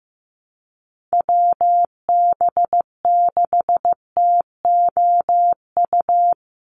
Morse code, heard as a keyed tone, about 15 wpm, 700 Hz. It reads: WB6TOU
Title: Call Signs Repeated 2x - 15wpm